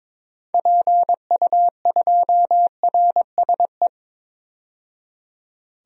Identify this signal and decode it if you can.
Morse code, heard as a keyed tone, about 22 wpm, 700 Hz. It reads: PU2RSE
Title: DXCC-100 Call Signs - 6 Characters - 22wpm